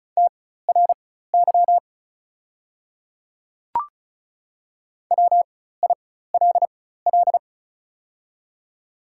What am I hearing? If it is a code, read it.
Morse code, heard as a keyed tone, about 36 wpm, spaced as for 16 wpm, 690 Hz. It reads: TRY WILL
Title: 4x Character Spacing - Top 300 Words - 35wpm